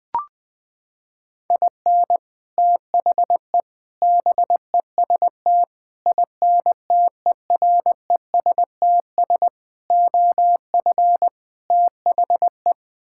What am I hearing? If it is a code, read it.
Morse code, heard as a keyed tone, about 20 wpm, 700 Hz. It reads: IN THE BEST INTERESTS OF THE
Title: Sets of 6 Words - 20wpm